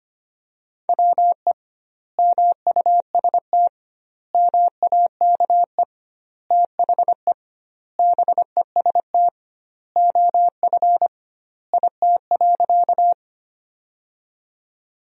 Morse code, heard as a keyed tone, about 25 wpm, 700 Hz. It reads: WE MUST MAKE THE BEST OF IT.
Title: 2x Word-Spacing - Sentences from Top 400 Words - 25wpm